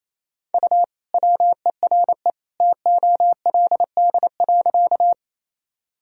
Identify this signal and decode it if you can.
Morse code, heard as a keyed tone, about 28 wpm, 700 Hz. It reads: U WERE TOLD.
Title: Sentences from Top 400 Words and Abbreviations 28wpm